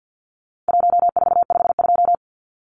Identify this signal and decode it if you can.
Morse code, heard as a keyed tone, about 50 wpm, 700 Hz. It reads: 1452
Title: Numbers - Contest Style - 4-Digit 50wpm